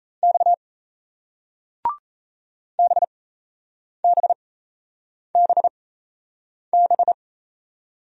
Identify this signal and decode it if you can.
Morse code, heard as a keyed tone, about 38 wpm, spaced as for 8 wpm, 700 Hz. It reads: X BBBB